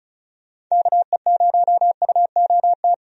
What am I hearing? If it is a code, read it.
Morse code, heard as a keyed tone, about 34 wpm, 700 Hz. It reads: KE0UOT
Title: USA Call Signs - 6 Characters - 35wpm